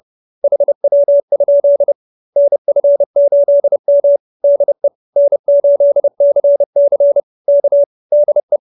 Morse code, heard as a keyed tone, about 30 wpm, 590 Hz. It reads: HW? NF8M DE N8CC K DE